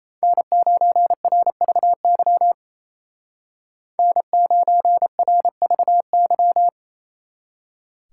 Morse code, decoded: N9RVY N9RVY

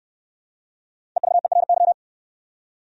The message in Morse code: EVERY

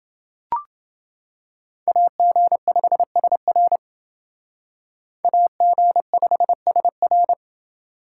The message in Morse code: AG5SR AG5SR

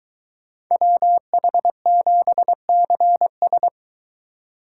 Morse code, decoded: WH7CS

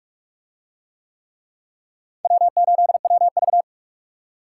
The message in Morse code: W8WV

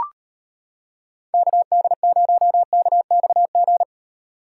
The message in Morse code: KD0KXG

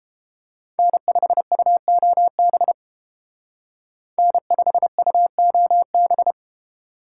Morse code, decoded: N5UOB N5UOB

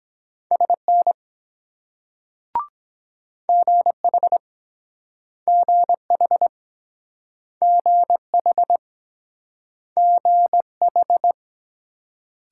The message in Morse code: SN GH GH GH GH

